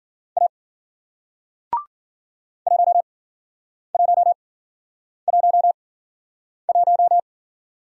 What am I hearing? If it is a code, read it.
Morse code, heard as a keyed tone, about 54 wpm, spaced as for 9 wpm, 700 Hz. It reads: A 1111